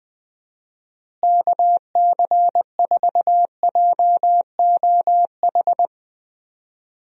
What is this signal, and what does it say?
Morse code, heard as a keyed tone, about 21 wpm, 700 Hz. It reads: KC4JOH
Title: USA Call Signs - 6 Characters - Repeated 2x - 20wpm